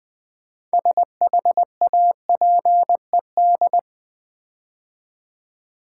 Morse code, heard as a keyed tone, about 20 wpm, 700 Hz. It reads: SHAPED
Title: Build A Word - 3 Letters At A Time - 6-Letter Words - 20wpm